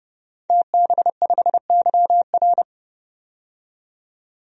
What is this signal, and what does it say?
Morse code, heard as a keyed tone, about 30 wpm, 700 Hz. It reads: TB5YR
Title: DXCC-100 Call Signs - 5 Characters - 30wpm